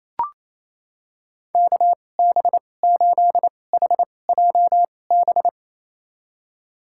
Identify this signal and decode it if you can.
Morse code, heard as a keyed tone, about 28 wpm, spaced as for 20 wpm, 700 Hz. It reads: KB8HJB
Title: USA Call Signs - 6 Characters - 2x Character Spacing - 28wpm